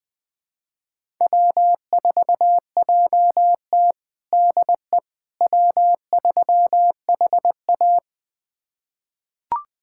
Morse code, heard as a keyed tone, about 21 wpm, 700 Hz. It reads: W4JT DE W3HA
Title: USA Callsign DE Callsign - 4 Characters - 20wpm